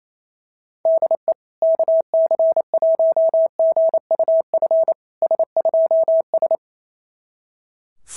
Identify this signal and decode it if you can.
Morse code, heard as a keyed tone, about 25 wpm, 650 Hz. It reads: DE KC1GUF S2S
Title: SOTA QSO Practice 28wpm